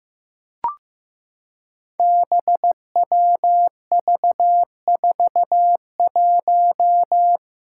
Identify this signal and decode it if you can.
Morse code, heard as a keyed tone, about 15 wpm, 700 Hz. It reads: BWV41
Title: Five Challenging Letter-Number Combinations 15wpm